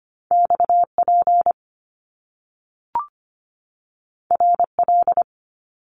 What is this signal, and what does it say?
Morse code, heard as a keyed tone, about 25 wpm, 700 Hz. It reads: XP RL
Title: Top Two Letter Combinations in Words 25wpm